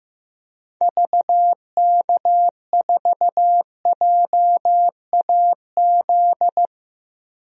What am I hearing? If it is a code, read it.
Morse code, heard as a keyed tone, about 15 wpm, 700 Hz. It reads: VK4JAZ
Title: DXCC-100 Call Signs - 6 Characters - Repeated 3x - 15wpm